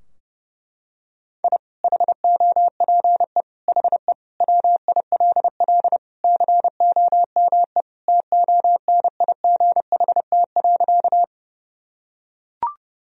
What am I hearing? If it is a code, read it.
Morse code, heard as a keyed tone, about 29 wpm, 700 Hz. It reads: I HOPE HE WILL COME TONIGHT.
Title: Sentences from Top 1000 Words - Review Part 3 - 30wpm